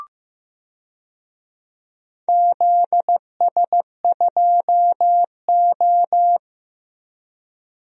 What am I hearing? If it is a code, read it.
Morse code, heard as a keyed tone, about 15 wpm, 700 Hz. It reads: ZS2O
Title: DXCC-100 Call Signs - 4 Characters - 15wpm